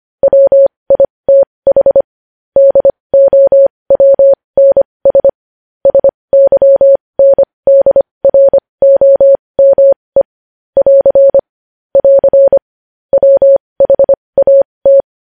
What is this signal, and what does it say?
Morse code, heard as a keyed tone, about 25 wpm, spaced as for 20 wpm, 550 Hz. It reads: WITH DOWNS SYNDROME <AR> <AR> WHAT
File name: podcast / News headlines-1x-25x20